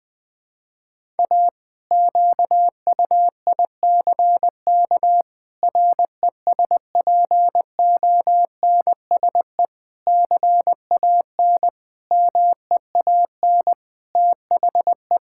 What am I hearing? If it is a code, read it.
Morse code, heard as a keyed tone, about 21 wpm, 700 Hz. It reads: A QUICK RESPONSE CAN MEAN THE
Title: Sentences with 8-Word Phrases - Spoken Beginning - 20wpm